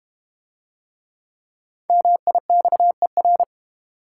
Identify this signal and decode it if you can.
Morse code, heard as a keyed tone, about 30 wpm, 690 Hz. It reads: MIXER
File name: Easy Words - Speed-Racing with Hints - 25wpm